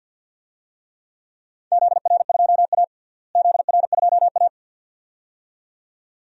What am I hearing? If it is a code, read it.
Morse code, heard as a keyed tone, about 51 wpm, 700 Hz. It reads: ZR2A ZR2A
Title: DXCC-100 Call Signs - 4 Characters - Repeated 2x - 50wpm